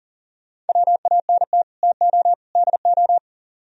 Morse code, heard as a keyed tone, about 52 wpm, 700 Hz. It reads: WANT TO DO